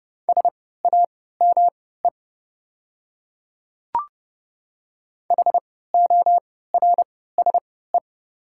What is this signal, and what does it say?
Morse code, heard as a keyed tone, about 29 wpm, spaced as for 17 wpm, 690 Hz. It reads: SAME HORSE
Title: 3x Character Spacing - Top 300 Words - 30wpm